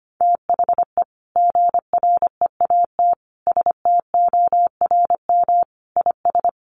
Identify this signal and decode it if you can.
Morse code, heard as a keyed tone, about 25 wpm, 700 Hz. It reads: THE GREAT STORM IS
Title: Sentences from Top 1500 Words 25wpm